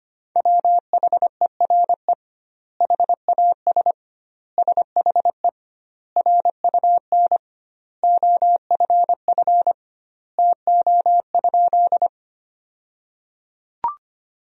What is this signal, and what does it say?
Morse code, heard as a keyed tone, about 25 wpm, 700 Hz. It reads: WHERE HAS SHE RUN OFF TO?
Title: 2x Word-Spacing - Sentences from Top 300 Words - 25wpm